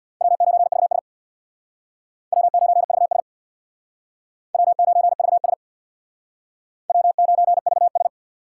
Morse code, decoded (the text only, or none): W9VS W9VS W9VS W9VS